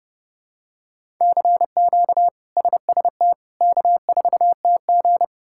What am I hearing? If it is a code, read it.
Morse code, heard as a keyed tone, about 30 wpm, 700 Hz. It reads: CQ SST K4TG
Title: K1USN SST QSO Practice - 30wpm